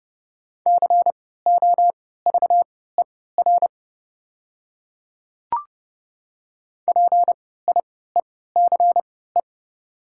Morse code, heard as a keyed tone, about 29 wpm, spaced as for 17 wpm, 700 Hz. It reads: COVER PIECE
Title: 3x Character Spacing - Top 500 Words - 30wpm